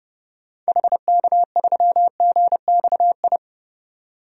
Morse code, heard as a keyed tone, about 30 wpm, 700 Hz. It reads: HK3GXI
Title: DXCC-100 Call Signs - 6 Characters - 30wpm